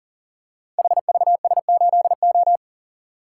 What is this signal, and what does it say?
Morse code, heard as a keyed tone, about 41 wpm, 700 Hz. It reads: HVS8O
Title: Code Groups - 5 Letter:Numbers - 40wpm